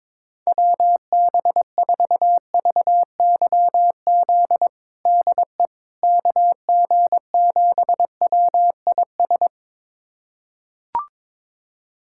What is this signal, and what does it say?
Morse code, heard as a keyed tone, about 22 wpm, 700 Hz. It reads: WB4VYZ DE KG7WIS
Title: USA Callsign DE Callsign 22wpm